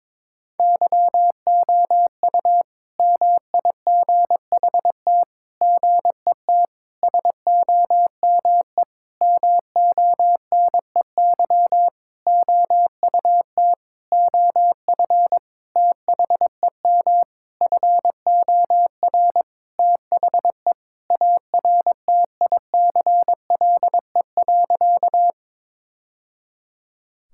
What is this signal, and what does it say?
Morse code, heard as a keyed tone, about 22 wpm, 700 Hz. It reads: YOU MIGHT GET SOME MONEY OUT OF THEM FOR THE ARTICLE.
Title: Sentences from Top 1000 Words - Review Part 5 - 22wpm